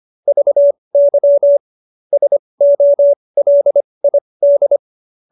Morse code, heard as a keyed tone, about 25 wpm, spaced as for 20 wpm, 590 Hz. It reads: VY SOLID
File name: QSO313 20